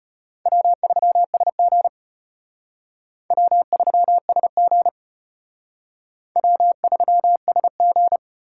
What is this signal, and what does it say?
Morse code, heard as a keyed tone, about 33 wpm, 700 Hz. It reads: W3SG W3SG W3SG